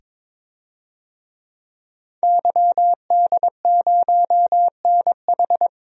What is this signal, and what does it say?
Morse code, heard as a keyed tone, about 21 wpm, 700 Hz. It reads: YD0NH
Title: DXCC-100 Call Signs - 5 Characters - 22wpm